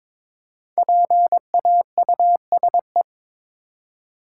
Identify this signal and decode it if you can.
Morse code, heard as a keyed tone, about 22 wpm, 690 Hz. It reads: PAUSE
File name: 5 Letter Words - First 3 Letters Given - 22wpm